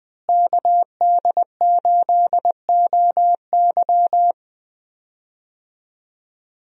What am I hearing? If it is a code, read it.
Morse code, heard as a keyed tone, about 20 wpm, 700 Hz. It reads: KD8OY
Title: USA Call Signs - 5 Characters - Repeated 3x - 20wpm